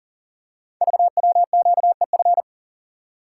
Morse code, heard as a keyed tone, about 38 wpm, 690 Hz. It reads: VWQEF
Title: Code Groups - 5 Letters - 40wpm